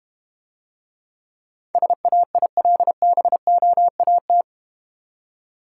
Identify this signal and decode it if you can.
Morse code, heard as a keyed tone, about 32 wpm, 700 Hz. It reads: SAILBOAT